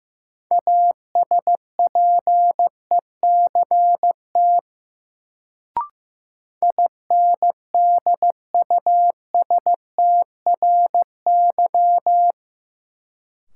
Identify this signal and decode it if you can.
Morse code, heard as a keyed tone, about 15 wpm, 700 Hz. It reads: ASPECT INDUSTRY
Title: Sentences from Top 1300 Words - New Word Drill Alt - 15wpm